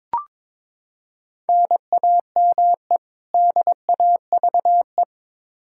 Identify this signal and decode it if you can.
Morse code, heard as a keyed tone, about 22 wpm, 700 Hz. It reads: NAME DAVE
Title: Practice QSO - Exchange 1 - Beginning - 22wpm